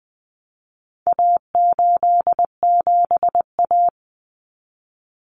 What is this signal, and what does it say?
Morse code, heard as a keyed tone, about 20 wpm, 700 Hz. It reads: A87A
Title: Common Cut Numbers - 4-Digit 20wpm